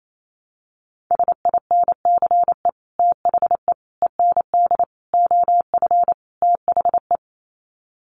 Morse code, heard as a keyed tone, about 28 wpm, 700 Hz. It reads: SINCE THE END OF THE
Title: Sets of 5 Words Encore 28wpm